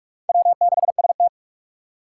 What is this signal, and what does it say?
Morse code, heard as a keyed tone, about 47 wpm, 700 Hz. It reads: W6ST
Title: USA Call Signs - 6x Character Spacing - 45wpm